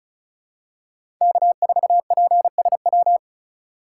Morse code, heard as a keyed tone, about 34 wpm, 700 Hz. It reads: K4PSW
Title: USA Call Signs - 4x Character Spacing - 35wpm